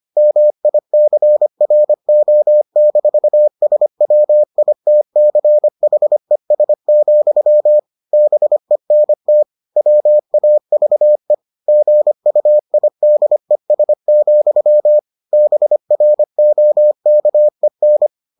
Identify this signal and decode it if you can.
Morse code, heard as a keyed tone, about 25 wpm, 600 Hz. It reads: MICRO-SWITCHES, BENT WAVE GUIDES, BROKEN